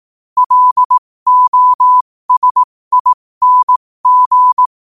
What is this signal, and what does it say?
Morse code, heard as a keyed tone, about 18 wpm, spaced as for 15 wpm, 1000 Hz. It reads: LOSING
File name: qotdfiles / 15wpm